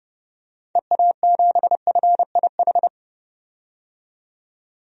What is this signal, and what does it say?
Morse code, heard as a keyed tone, about 30 wpm, 700 Hz. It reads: EA7FIH
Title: DXCC-100 Call Signs - 6 Characters - 30wpm